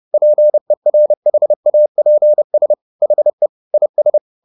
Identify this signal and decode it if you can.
Morse code, heard as a keyed tone, about 30 wpm, 600 Hz. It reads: PERHAPS HE IS
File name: the old man and the sea / the old man and the sea 1 30wpm